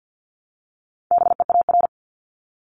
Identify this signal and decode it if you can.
Morse code, heard as a keyed tone, about 50 wpm, 690 Hz. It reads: BEAR
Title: Multi-letter Phonemes 50wpm